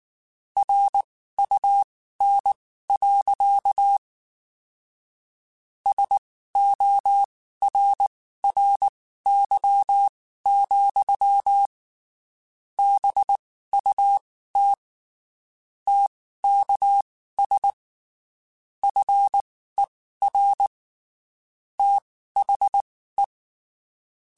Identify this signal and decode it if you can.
Morse code, heard as a keyed tone, about 19 wpm, spaced as for 14 wpm, 800 Hz. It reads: RUN. SORRY, BUT TKS FER THE
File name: (spaced) / QSOs Spaced-mp3s / kc7sac @18 sp